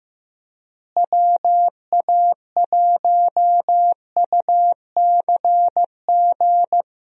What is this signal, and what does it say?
Morse code, heard as a keyed tone, about 15 wpm, 700 Hz. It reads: WA1UCG
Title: USA Call Signs - 6 Characters - Repeated 2x - 15wpm